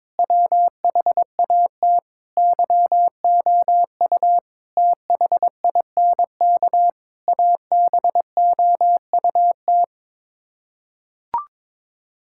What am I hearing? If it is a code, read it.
Morse code, heard as a keyed tone, about 22 wpm, 700 Hz. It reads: WHAT YOU THINK ABOUT
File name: practice / 22wpm / Sets of 8 Words - 22wpm